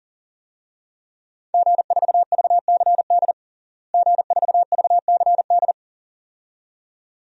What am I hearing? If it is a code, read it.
Morse code, heard as a keyed tone, about 40 wpm, 700 Hz. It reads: G4VCD G4VCD
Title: DXCC-100 Call Signs - 5 Characters - Repeated 2x - 40wpm